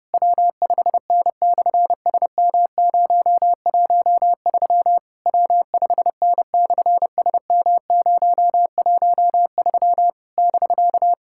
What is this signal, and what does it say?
Morse code, heard as a keyed tone, about 28 wpm, 700 Hz. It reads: W5N/SM013 W5N/SM013 <BK>
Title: SOTA QSO Practice 30wpm